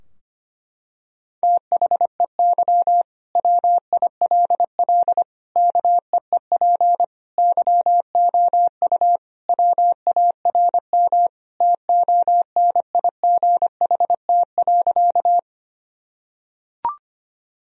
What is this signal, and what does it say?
Morse code, heard as a keyed tone, about 25 wpm, 700 Hz. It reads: THEY WILL KEEP YOU WARM TONIGHT.
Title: Sentences from Top 1000 Words - Review Part 3 - 25wpm